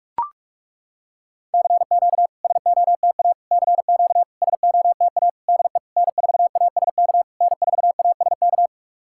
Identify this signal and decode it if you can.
Morse code, heard as a keyed tone, about 41 wpm, 690 Hz. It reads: CQ SOTA CQ SOTA DE N4ASK N4ASK
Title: SOTA QSO Practice 45wpm